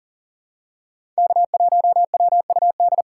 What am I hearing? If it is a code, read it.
Morse code, heard as a keyed tone, about 40 wpm, 700 Hz. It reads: K1WUD
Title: USA Call Signs - 5 Characters - 4x Character Spacing - 40wpm